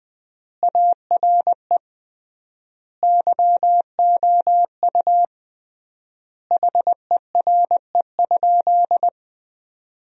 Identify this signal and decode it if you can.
Morse code, heard as a keyed tone, about 20 wpm, 700 Hz. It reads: ARE YOU HERE?
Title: 3x Word-Spacing - Sentences from Top 200 Words - 20wpm